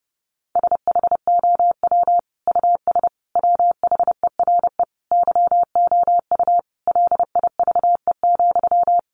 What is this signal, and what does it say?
Morse code, heard as a keyed tone, about 30 wpm, 700 Hz. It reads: SHOW US WHERE YOU LIVE,
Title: Sentences from Top 200 Words 30wpm